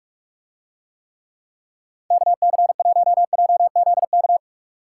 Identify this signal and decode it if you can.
Morse code, heard as a keyed tone, about 41 wpm, 700 Hz. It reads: KC1JZK